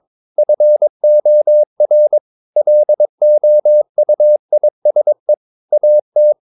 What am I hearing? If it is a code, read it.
Morse code, heard as a keyed tone, about 22 wpm, 600 Hz. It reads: FOR LOUISE AT
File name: Ferie QRS 22wpm